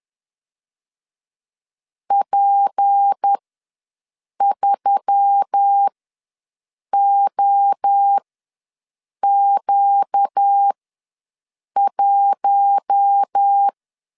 Morse code, heard as a keyed tone, about 10 wpm, spaced as for 6 wpm, 800 Hz. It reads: P3OQ1
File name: LEZIONI / 6 PKX0 / 6 P ws 10wpm 3751